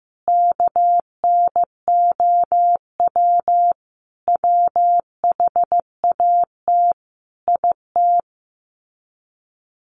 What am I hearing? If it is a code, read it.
Morse code, heard as a keyed tone, about 15 wpm, 700 Hz. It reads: KNOW WHAT IT
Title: Sets of 5 Words 15wpm